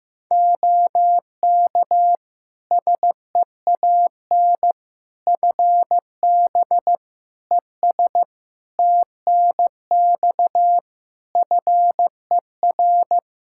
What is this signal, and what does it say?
Morse code, heard as a keyed tone, about 15 wpm, 700 Hz. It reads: OK SEAN FB ES TNX FER
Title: Practice QSO - Exchange 2 - Info - 15wpm